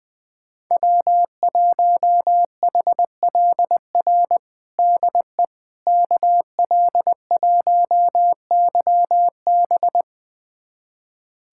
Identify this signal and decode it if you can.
Morse code, heard as a keyed tone, about 20 wpm, 700 Hz. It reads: W1HLR DE KL1YB
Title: USA Callsign DE Callsign - 5 Characters - 20wpm